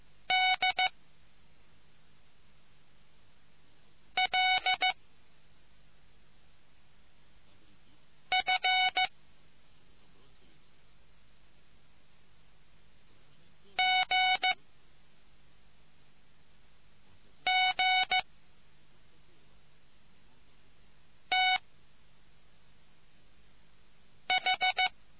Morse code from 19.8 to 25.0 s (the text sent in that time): TH